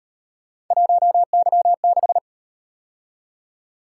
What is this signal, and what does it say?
Morse code, heard as a keyed tone, about 37 wpm, 700 Hz. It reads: 1Y6